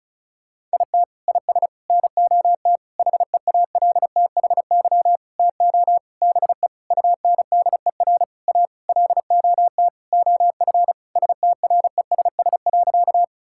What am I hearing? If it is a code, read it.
Morse code, heard as a keyed tone, about 36 wpm, 700 Hz. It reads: IT IS NOT HEALTHY TO BE UNDER A LOT OF STRESS.
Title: Sentences from Top 1000 Words - Review Part 6 - 35wpm